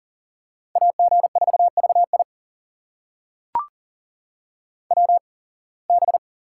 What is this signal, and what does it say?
Morse code, heard as a keyed tone, about 39 wpm, spaced as for 26 wpm, 700 Hz. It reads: AG4VI WB
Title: USA Call Signs - 8x Character Spacing - 40wpm